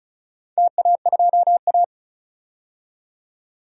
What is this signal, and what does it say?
Morse code, heard as a keyed tone, about 36 wpm, 700 Hz. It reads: TA2A
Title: DXCC-100 Call Signs - 4 Characters - 35wpm